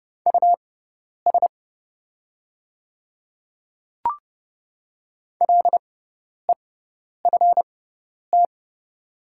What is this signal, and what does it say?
Morse code, heard as a keyed tone, about 30 wpm, spaced as for 10 wpm, 700 Hz. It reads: US LEFT
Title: 6x Character Spacing - Top 300 Words - 30wpm